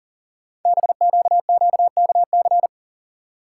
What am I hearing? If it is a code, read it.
Morse code, heard as a keyed tone, about 40 wpm, 700 Hz. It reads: BQQKC
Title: Code Groups - 5 Letters - 40wpm